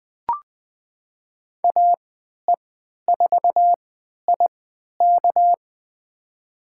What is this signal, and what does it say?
Morse code, heard as a keyed tone, about 20 wpm, spaced as for 11 wpm, 700 Hz. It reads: AE4IK
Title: USA Call Signs - 5 Characters - 3x Character Spacing - 20wpm